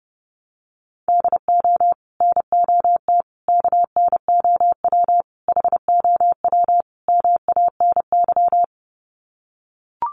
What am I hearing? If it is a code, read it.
Morse code, heard as a keyed tone, about 30 wpm, 700 Hz. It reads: DO NOT KNOW HOW MANY
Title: Sets of 5 Words Encore 30wpm